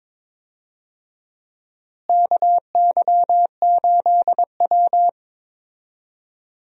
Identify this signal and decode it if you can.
Morse code, heard as a keyed tone, about 22 wpm, 700 Hz. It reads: KY8W
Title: USA Call Signs - 4 Characters - Repeated 3x - 22wpm